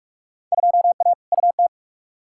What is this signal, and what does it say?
Morse code, heard as a keyed tone, about 41 wpm, 700 Hz. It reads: 2A UT
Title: Field Day Exchanges - 45wpm